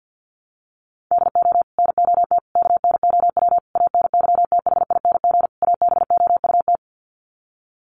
Morse code, heard as a keyed tone, about 50 wpm, 700 Hz. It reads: DO NOT KNOW ANYTHING ABOUT
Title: Sets of 5 Words 50wpm